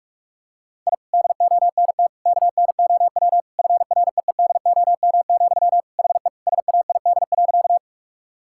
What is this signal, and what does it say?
Morse code, heard as a keyed tone, about 46 wpm, 700 Hz. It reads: I DONT KNOW FREEDOM, HE SAID.
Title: Sentences from Top 1000 Words - Review Part 5 - 45wpm